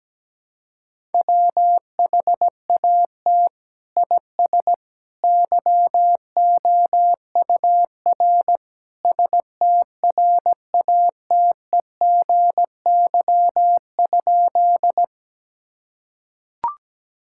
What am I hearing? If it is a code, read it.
Morse code, heard as a keyed tone, about 17 wpm, 700 Hz. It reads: WHAT IS YOUR STRATEGY?
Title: Sentences from Top 1000 Words - Review Part 5 - 17wpm